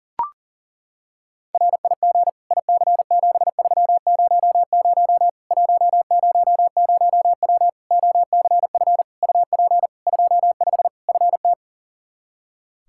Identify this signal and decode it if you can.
Morse code, heard as a keyed tone, about 40 wpm, 700 Hz. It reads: RIG IC7300 100W OCF UP 25 FT
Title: Practice QSO - Exchange 2 - Info - 40wpm